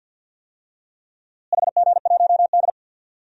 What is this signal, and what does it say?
Morse code, heard as a keyed tone, about 47 wpm, 700 Hz. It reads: HG1D
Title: DXCC-100 Call Signs - 4 Characters - Repeated 2x - 50wpm